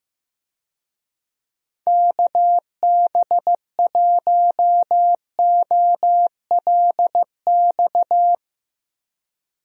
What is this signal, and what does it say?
Morse code, heard as a keyed tone, about 15 wpm, 700 Hz. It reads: KB1OLX
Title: USA Call Signs - 6 Characters - Repeated 2x - 15wpm